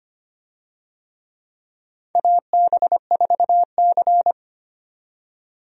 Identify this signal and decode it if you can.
Morse code, heard as a keyed tone, about 25 wpm, 700 Hz. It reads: AB4C
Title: USA Call Signs - 4 Characters - 25wpm